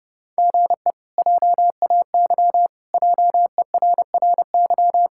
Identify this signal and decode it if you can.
Morse code, heard as a keyed tone, about 29 wpm, 700 Hz. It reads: GE JAY JERRY
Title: K1USN SST QSO Practice - 30wpm